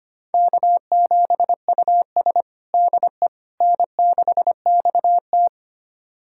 Morse code, heard as a keyed tone, about 25 wpm, 700 Hz. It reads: K7US DE N6XT
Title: USA Callsign DE Callsign - 4 Characters - 25wpm